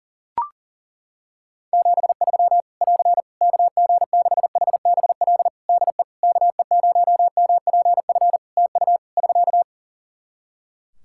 Morse code, heard as a keyed tone, about 40 wpm, 700 Hz. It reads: 73 <AR> KG6HBL DE KE0MPF TU <SK>
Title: Practice QSO - Exchange 4 - Ending - 40wpm